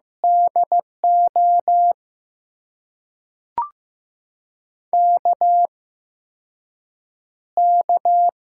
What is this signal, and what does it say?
Morse code, heard as a keyed tone, about 15 wpm, spaced as for 6 wpm, 700 Hz. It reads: DO KK
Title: USA Call Signs - 6 Characters - 8x Character Spacing - 15wpm